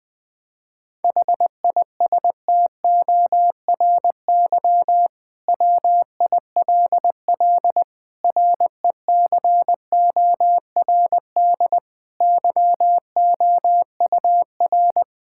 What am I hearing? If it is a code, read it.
Morse code, heard as a keyed tone, about 20 wpm, 700 Hz. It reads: HISTORY WILL RECORD YOUR
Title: Sentences from Top 1600 Words 20wpm